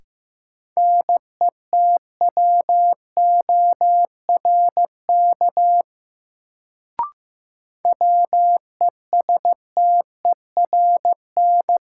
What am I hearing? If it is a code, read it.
Morse code, heard as a keyed tone, about 15 wpm, 700 Hz. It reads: NETWORK WESTERN
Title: Sentences from Top 1000 Words - Review Part 7 - New Word Drill - 15wpm